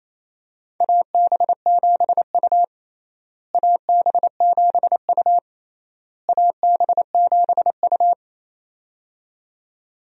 Morse code, heard as a keyed tone, about 28 wpm, 700 Hz. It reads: AB7U AB7U AB7U
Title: USA Call Signs - 4 Characters - Repeated 3x - 28wpm